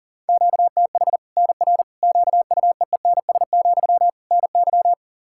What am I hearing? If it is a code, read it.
Morse code, heard as a keyed tone, about 37 wpm, 700 Hz. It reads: QTH NR QUEENS, NY
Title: Practice QSO - Exchange 1 - Beginning - 40wpm